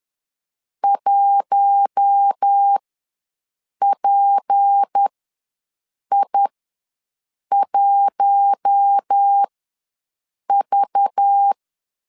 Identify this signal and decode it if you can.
Morse code, heard as a keyed tone, about 10 wpm, spaced as for 6 wpm, 800 Hz. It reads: 1PI1V
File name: LEZIONI / 6 PKX0 / 6 PK ws 10wpm 3745